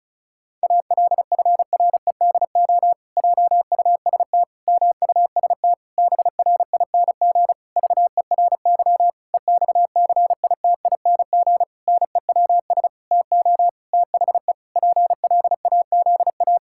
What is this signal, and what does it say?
Morse code, heard as a keyed tone, about 36 wpm, 700 Hz. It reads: ALFREDO JUST MUST BRING VERY EXCITING NEWS TO THE PLAZA
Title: Pangram Challenge v2 - 35wpm